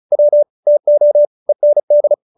Morse code, heard as a keyed tone, about 34 wpm, 590 Hz. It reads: W TO END
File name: QSO313 35